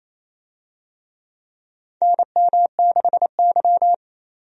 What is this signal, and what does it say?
Morse code, heard as a keyed tone, about 27 wpm, 700 Hz. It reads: NM6Y
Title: USA Call Signs - 4 Characters - 28wpm